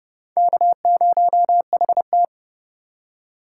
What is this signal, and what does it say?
Morse code, heard as a keyed tone, about 30 wpm, 700 Hz. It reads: K0HT
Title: K1USN SST QSO Practice - 30wpm